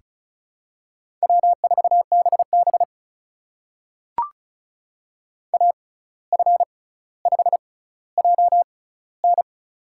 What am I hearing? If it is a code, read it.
Morse code, heard as a keyed tone, about 36 wpm, spaced as for 17 wpm, 700 Hz. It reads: W4BB AF5JN
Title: USA Call Signs - 6x Character Spacing - 35wpm